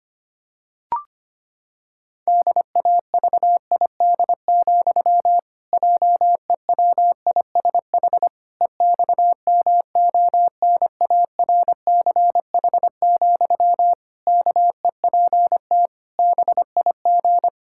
Morse code, heard as a keyed tone, about 25 wpm, 700 Hz. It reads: DAVID, JEWISH EXMONARCH, KEPT BIG
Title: Pangram Challenge v2 - 25wpm